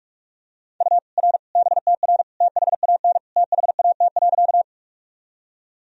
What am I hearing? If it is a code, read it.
Morse code, heard as a keyed tone, about 46 wpm, 700 Hz. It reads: U R BTR THAN THAT.
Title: Sentences from Top 400 Words and Abbreviations 45wpm